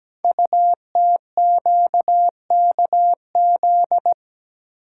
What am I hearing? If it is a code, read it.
Morse code, heard as a keyed tone, about 17 wpm, 700 Hz. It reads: UTQKZ
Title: Code Groups - 5 Letters - 17wpm